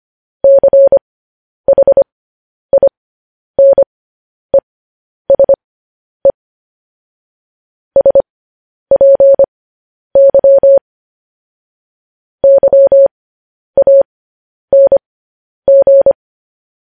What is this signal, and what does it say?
Morse code, heard as a keyed tone, about 25 wpm, spaced as for 10 wpm, 550 Hz. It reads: CHINESE SPY YANG
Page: news headlines in morse code at 10 wpm